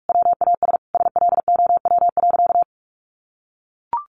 Morse code, heard as a keyed tone, about 44 wpm, 700 Hz. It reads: WAS SLOW.
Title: Sentences from Top 1400 Words 45wpm